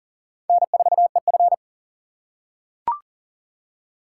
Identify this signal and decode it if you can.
Morse code, heard as a keyed tone, about 39 wpm, 700 Hz. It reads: N4EF
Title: USA Call Signs - 4 Characters - 3x Character Spacing - 40wpm